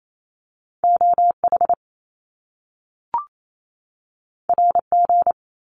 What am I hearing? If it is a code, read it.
Morse code, heard as a keyed tone, about 28 wpm, 700 Hz. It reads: OH RG
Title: Top Two Letter Combinations in Words 28wpm